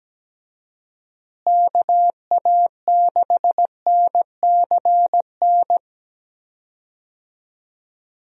Morse code, heard as a keyed tone, about 17 wpm, 700 Hz. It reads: KA6NCN
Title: USA Call Signs - 6 Characters - Repeated 2x - 17wpm